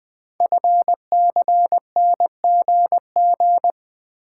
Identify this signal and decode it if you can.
Morse code, heard as a keyed tone, about 20 wpm, 700 Hz. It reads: FCNGG
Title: Code Groups - 5 Letters - 20wpm